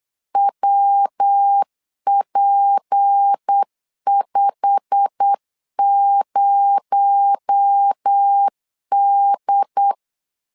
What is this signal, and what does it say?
Morse code, heard as a keyed tone, about 8 wpm, 800 Hz. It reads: WP50D
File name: LEZIONI / 10 SIGNS / 10C DOT 8wpm 3649